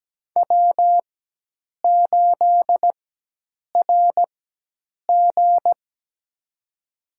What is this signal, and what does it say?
Morse code, heard as a keyed tone, about 17 wpm, spaced as for 8 wpm, 700 Hz. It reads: W8RG
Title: USA Call Signs - 4 Characters - 4x Character Spacing - 17wpm